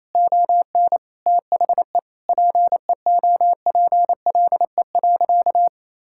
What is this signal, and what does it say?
Morse code, heard as a keyed tone, about 28 wpm, 700 Hz. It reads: ON THE PEOPLE.
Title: Sentences from Top 1000 Words - Review Part 5 - 28wpm